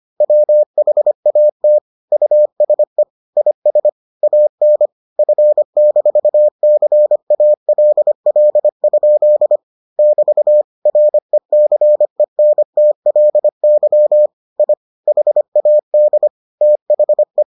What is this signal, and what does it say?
Morse code, heard as a keyed tone, about 25 wpm, 600 Hz. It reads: WHAT USE IS AN F-CALL? <BT> RECENTLY I HAD THE